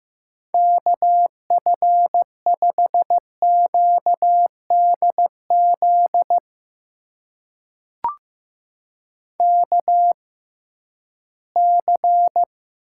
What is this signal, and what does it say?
Morse code, heard as a keyed tone, about 15 wpm, spaced as for 11 wpm, 700 Hz. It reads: KF5QDZ KC
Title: USA Call Signs - 6 Characters - 6x Character Spacing - 15wpm